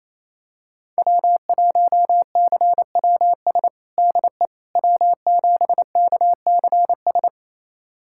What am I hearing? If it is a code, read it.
Morse code, heard as a keyed tone, about 28 wpm, 700 Hz. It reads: W1CWS DE W7KCS
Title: USA Callsign DE Callsign - 5 Characters - 28wpm